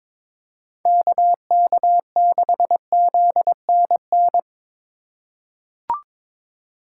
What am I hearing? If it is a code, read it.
Morse code, heard as a keyed tone, about 22 wpm, 700 Hz. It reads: KK6ZNN
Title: USA Call Signs - 6 Characters - 2x Character Spacing - 22wpm